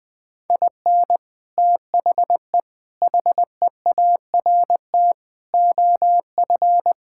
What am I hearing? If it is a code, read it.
Morse code, heard as a keyed tone, about 20 wpm, 700 Hz. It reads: IN THE HEART OF
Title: Sentences with 3-Word Phrases - Spoken Beginning - 20wpm